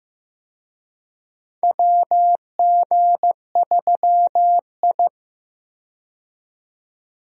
Morse code, heard as a keyed tone, about 15 wpm, 700 Hz. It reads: WG3I
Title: USA Call Signs - 4 Characters - 15wpm